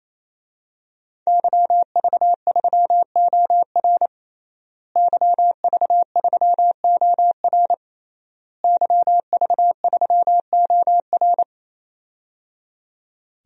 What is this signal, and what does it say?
Morse code, heard as a keyed tone, about 28 wpm, 700 Hz. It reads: YV3OR YV3OR YV3OR
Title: DXCC-100 Call Signs - 5 Characters - Repeated 3x - 28wpm